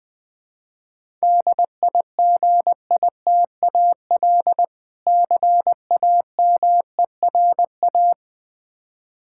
Morse code, heard as a keyed tone, about 20 wpm, 700 Hz. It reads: DIGITAL CAMERA
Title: Easy Words - Speed-Racing with Hints - 20wpm